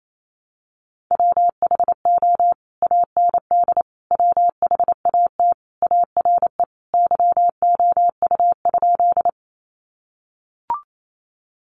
Morse code, heard as a keyed tone, about 28 wpm, 700 Hz. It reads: WHO AND WHAT ARE YOU?
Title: Sentences from Top 100 Words 28wpm